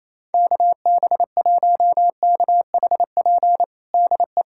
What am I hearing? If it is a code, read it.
Morse code, heard as a keyed tone, about 28 wpm, 700 Hz. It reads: KB1KHP DE
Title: USA Callsign DE Callsign 28wpm